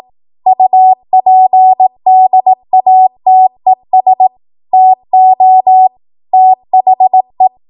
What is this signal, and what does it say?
Morse code, heard as a keyed tone, about 18 wpm, 750 Hz. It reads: UPDATES TO THE